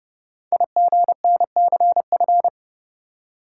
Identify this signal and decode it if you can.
Morse code, heard as a keyed tone, about 28 wpm, 700 Hz. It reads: IGNCF
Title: Code Groups - 5 Letters - 30wpm